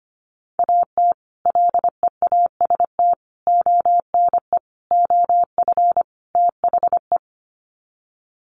Morse code, heard as a keyed tone, about 25 wpm, 700 Hz. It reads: AT LEAST ONE OF THE
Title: Sets of 5 Words Encore 25wpm